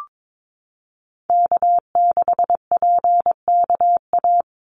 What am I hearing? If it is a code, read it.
Morse code, heard as a keyed tone, about 22 wpm, 700 Hz. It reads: K6PKA
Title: USA Call Signs 22wpm